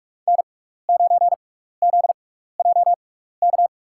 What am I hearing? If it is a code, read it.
Morse code, heard as a keyed tone, about 46 wpm, spaced as for 15 wpm, 700 Hz. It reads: N9ZJK
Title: USA Call Signs - 5 Characters - 6x Character Spacing - 45wpm